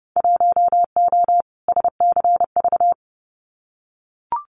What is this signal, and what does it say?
Morse code, heard as a keyed tone, about 30 wpm, 700 Hz. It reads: 1O SCV
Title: Winter Field Day Exchanges 30wpm